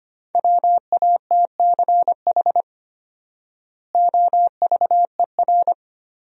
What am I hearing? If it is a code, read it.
Morse code, heard as a keyed tone, about 25 wpm, 700 Hz. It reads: WATCH OVER
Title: 4x Word-Spacing - Sentences from Top 300 Words - 25wpm